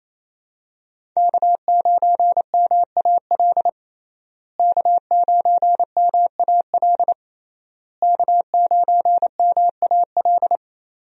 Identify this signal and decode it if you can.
Morse code, heard as a keyed tone, about 28 wpm, 700 Hz. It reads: K9MAL K9MAL K9MAL
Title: USA Call Signs - 5 Characters - Repeated 3x - 28wpm